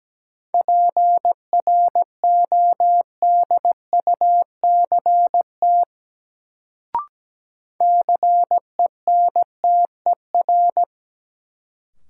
Morse code, heard as a keyed tone, about 17 wpm, 700 Hz. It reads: PRODUCT CENTER
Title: Sentences from Top 1000 Words - Review Part 2 - New Word Drill - 17wpm